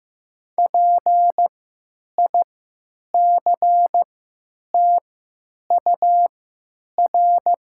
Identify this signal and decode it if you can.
Morse code, heard as a keyed tone, about 15 wpm, spaced as for 9 wpm, 700 Hz. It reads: PICTUR
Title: 3x Character Spacing - Top 300 Words - 15wpm